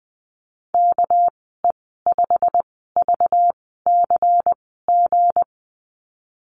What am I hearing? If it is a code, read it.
Morse code, heard as a keyed tone, about 20 wpm, spaced as for 15 wpm, 700 Hz. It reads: KE5VCG
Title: 200 Lesson 32g - Letter k - Call Signs 20wpm